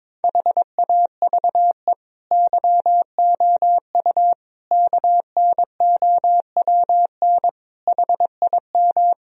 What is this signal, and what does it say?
Morse code, heard as a keyed tone, about 22 wpm, 700 Hz. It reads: HAVE YOU KNOWN HIM
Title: Sentences from Top 1000 Words - Review Part 3 - 22wpm